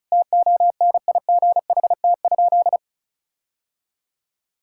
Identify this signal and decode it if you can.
Morse code, heard as a keyed tone, about 36 wpm, 700 Hz. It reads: TONIGHT?
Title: Sentences from Top 1600 Words 35wpm